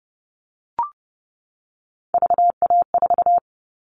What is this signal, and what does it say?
Morse code, heard as a keyed tone, about 30 wpm, 700 Hz. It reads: VA4
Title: All Cut Numbers - 3-Digit 30wpm